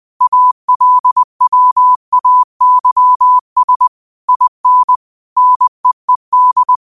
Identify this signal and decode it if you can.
Morse code, heard as a keyed tone, about 20 wpm, 1000 Hz. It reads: ALWAYS IN NEED
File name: qotdfiles / 20wpm